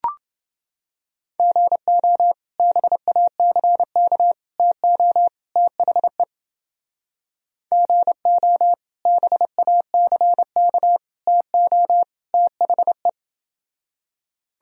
Morse code, heard as a keyed tone, about 29 wpm, 700 Hz. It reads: GO BACK TO THE GO BACK TO THE